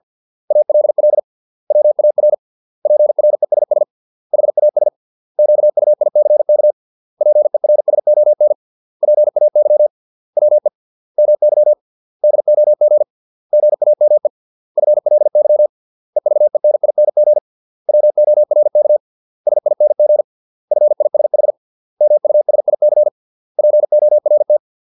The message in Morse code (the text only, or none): ALL WAR PRESS HAS QUICK PERSON PAY WE MY DOG GAME FLY EVENING WORK SING FISH MUSIC PORT